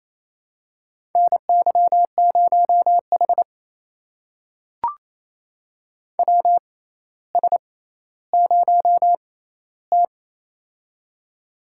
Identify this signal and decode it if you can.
Morse code, heard as a keyed tone, about 27 wpm, spaced as for 14 wpm, 700 Hz. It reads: NY0H WS0T
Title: USA Call Signs - 4 Characters - 6x Character Spacing - 28wpm